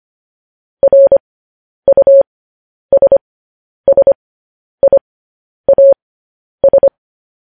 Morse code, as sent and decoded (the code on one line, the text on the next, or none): .-. ..- ... ... .. .- ...
RUSSIAS